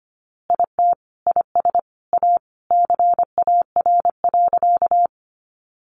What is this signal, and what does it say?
Morse code, heard as a keyed tone, about 25 wpm, 700 Hz. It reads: IT IS A CAR.
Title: Easy Phrases 25wpm